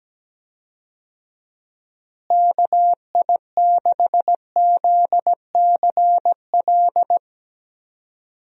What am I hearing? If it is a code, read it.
Morse code, heard as a keyed tone, about 17 wpm, 700 Hz. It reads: KI6ZCL
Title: USA Call Signs - 6 Characters - Repeated 2x - 17wpm